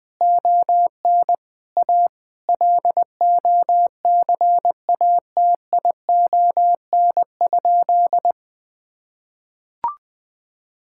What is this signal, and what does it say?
Morse code, heard as a keyed tone, about 21 wpm, 700 Hz. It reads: ON A LOCATION?
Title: Sentences from Top 1000 Words - Review Part 6 - 20wpm